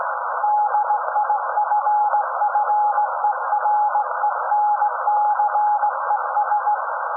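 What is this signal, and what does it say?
Morse code, heard as a keyed tone, about 17 wpm, 840 Hz, with noise. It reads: DIFFERENCES
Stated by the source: SAQ Grimeton Radio, Alexanderson 200 kW alternator on 17.2 kHz, received in Potsdam with a very strong signal on World Radio Day.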